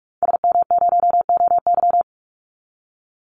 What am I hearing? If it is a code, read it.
Morse code, heard as a keyed tone, about 44 wpm, 700 Hz. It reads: SM0OY
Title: Call Signs Repeated 2x - 45wpm